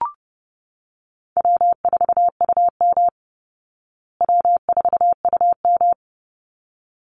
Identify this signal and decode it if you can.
Morse code, heard as a keyed tone, about 29 wpm, 700 Hz. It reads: W4UM W4UM
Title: Easy Call Signs Repeated 2x - 30wpm